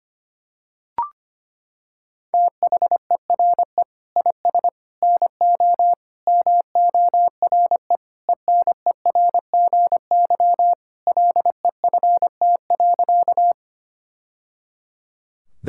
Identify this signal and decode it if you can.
Morse code, heard as a keyed tone, about 25 wpm, 700 Hz. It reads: THERE IS NO MORE ENERGY LEFT.
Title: Sentences from Top 1000 Words - Review Part 6 - 25wpm